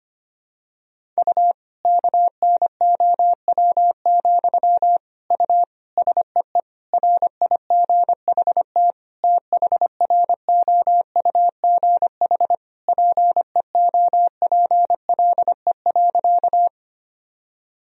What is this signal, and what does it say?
Morse code, heard as a keyed tone, about 25 wpm, 700 Hz. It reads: U KNOW, U SEE RIGHT THROUGH PEOPLE.
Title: Sentences from Top 200 Words and Abbreviations 25wpm